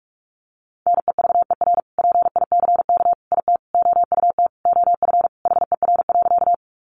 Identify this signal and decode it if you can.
Morse code, heard as a keyed tone, about 44 wpm, 700 Hz. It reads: NEVER PICK IT OUT OF HER.
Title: Sentences from Top 1000 Words 45wpm